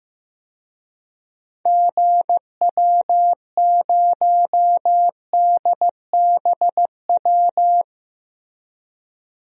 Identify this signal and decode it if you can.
Morse code, heard as a keyed tone, about 15 wpm, 700 Hz. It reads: GW0DBW
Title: DXCC-100 Call Signs - 6 Characters - 15wpm